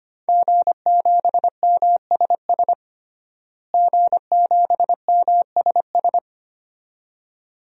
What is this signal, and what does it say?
Morse code, heard as a keyed tone, about 25 wpm, 700 Hz. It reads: G7MSS G7MSS
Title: DXCC-100 Call Signs - 5 Characters - Repeated 2x - 25wpm